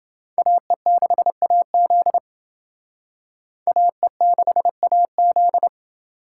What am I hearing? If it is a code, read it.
Morse code, heard as a keyed tone, about 28 wpm, 700 Hz. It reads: AE6AZ AE6AZ